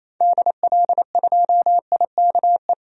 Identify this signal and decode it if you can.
Morse code, heard as a keyed tone, about 27 wpm, 700 Hz. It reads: DL2IKE
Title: DXCC-100 Call Signs - 6 Characters - 28wpm